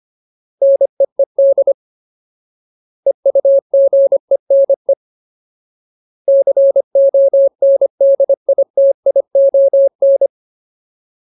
Morse code, heard as a keyed tone, about 25 wpm, 550 Hz. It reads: NEED EUGENE CONDITION
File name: Ultimate Word List / 25WPM Ultimate